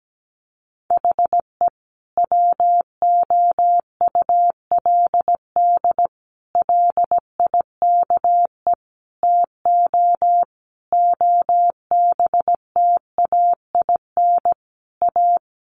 Morse code, heard as a keyed tone, about 17 wpm, 700 Hz. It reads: HE WOULD LIKE TO OBTAIN A
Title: Sentences from Top 1300 Words 17wpm